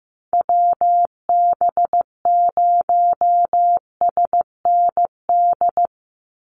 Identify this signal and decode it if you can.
Morse code, heard as a keyed tone, about 15 wpm, 700 Hz. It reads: WB0SND
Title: Call Signs Repeated 3x - 15wpm